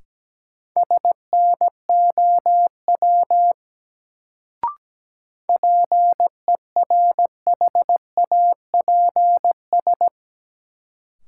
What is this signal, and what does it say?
Morse code, heard as a keyed tone, about 17 wpm, 700 Hz. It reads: SNOW PERHAPS
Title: Sentences from Top 500 Words - New Word Drill - 17wpm